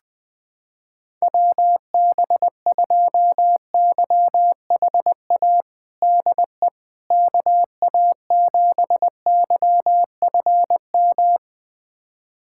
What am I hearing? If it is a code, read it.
Morse code, heard as a keyed tone, about 21 wpm, 700 Hz. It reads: WB2YHA DE KA7YFM
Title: USA Callsign DE Callsign - 6 Characters - 20wpm